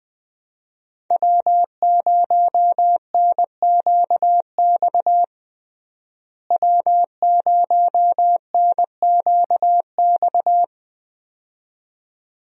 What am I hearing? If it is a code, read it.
Morse code, heard as a keyed tone, about 20 wpm, 700 Hz. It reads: W0NQX W0NQX
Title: USA Call Signs - 5 Characters - Repeated 2x - 20wpm